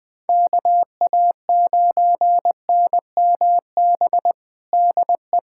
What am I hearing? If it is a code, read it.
Morse code, heard as a keyed tone, about 20 wpm, 700 Hz. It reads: KA9NMB DE
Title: USA Callsign DE Callsign 20wpm